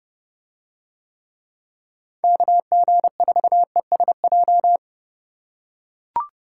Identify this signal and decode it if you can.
Morse code, heard as a keyed tone, about 30 wpm, 700 Hz. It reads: KG4ESJ